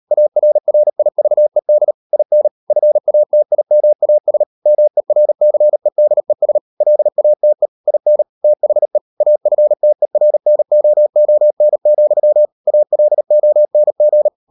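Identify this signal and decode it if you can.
Morse code, heard as a keyed tone, about 38 wpm, 600 Hz. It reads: ARRIVED IN FATIMAS MERCEDES LATE IN THE AFTERNOON, ALONG
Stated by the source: Morse code practice at 38 WPM, sending Ken Follett's novel Never in English, from CW med Gnister.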